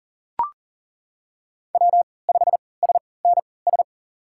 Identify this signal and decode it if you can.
Morse code, heard as a keyed tone, about 40 wpm, spaced as for 23 wpm, 690 Hz. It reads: W5SNS
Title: USA Call Signs - 5 Characters - 3x Character Spacing - 40wpm